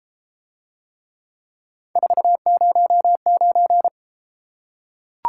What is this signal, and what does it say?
Morse code, heard as a keyed tone, about 33 wpm, 700 Hz. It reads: 409